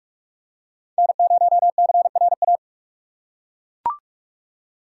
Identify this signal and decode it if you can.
Morse code, heard as a keyed tone, about 41 wpm, 700 Hz. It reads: N0CRA
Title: USA Call Signs - 3x Character Spacing - 45wpm